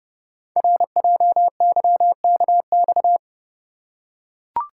Morse code, heard as a keyed tone, about 30 wpm, 700 Hz. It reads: RJYKX
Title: Code Groups - 5 Letters - 30wpm